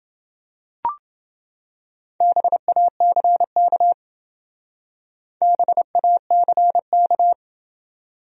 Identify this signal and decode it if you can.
Morse code, heard as a keyed tone, about 29 wpm, 700 Hz. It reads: BACK BACK